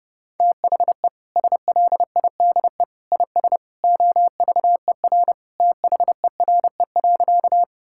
Morse code, heard as a keyed tone, about 32 wpm, 700 Hz. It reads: THE SLIDE IS OVER THERE.
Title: Sentences from Top 1600 Words 30wpm